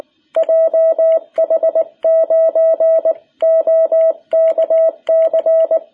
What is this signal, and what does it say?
Morse code, heard as a keyed tone, about 19 wpm, 660 Hz. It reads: JH9OKC
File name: jh8cbh / 2021cw